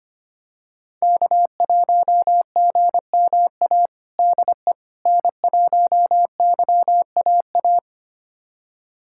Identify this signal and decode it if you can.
Morse code, heard as a keyed tone, about 25 wpm, 700 Hz. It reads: K1GMA DE N1YAA
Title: USA Callsign DE Callsign - 5 Characters - 25wpm